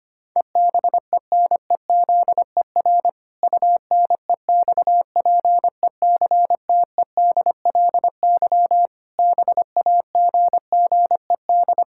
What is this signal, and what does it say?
Morse code, heard as a keyed tone, about 25 wpm, 700 Hz. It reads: EBENEZER UNEXPECTEDLY BAGGED
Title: Pangram Challenge v2 - 25wpm